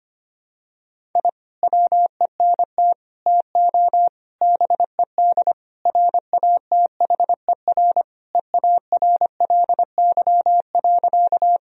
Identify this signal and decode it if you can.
Morse code, heard as a keyed tone, about 25 wpm, 700 Hz. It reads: I WENT TO BED RATHER EARLY.
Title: Sentences from Top 1000 Words - Review Part 4 - 25wpm